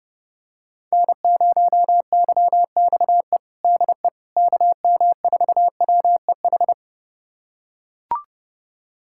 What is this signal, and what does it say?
Morse code, heard as a keyed tone, about 30 wpm, 700 Hz. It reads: N0YXE DE KM4WEH
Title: USA Callsign DE Callsign 30wpm